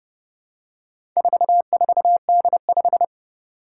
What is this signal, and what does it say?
Morse code, heard as a keyed tone, about 28 wpm, 700 Hz. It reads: 44D5
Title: Four Challenging Letter-Number Combinations 30wpm